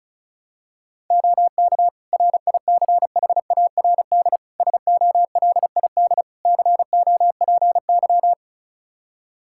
Morse code, heard as a keyed tone, about 36 wpm, 700 Hz. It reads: OK RICHARD SOLID COPY
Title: Practice QSO - Exchange 3 - More Info - 35wpm